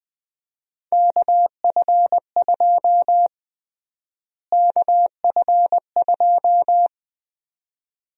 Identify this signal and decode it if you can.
Morse code, heard as a keyed tone, about 21 wpm, 700 Hz. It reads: KF2 KF2
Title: USA Call Sign Prefixes - Repeated 2x - 20wpm